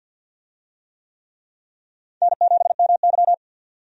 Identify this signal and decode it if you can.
Morse code, heard as a keyed tone, about 52 wpm, 700 Hz. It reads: N7MY
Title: USA Call Signs - 4 Characters - 50wpm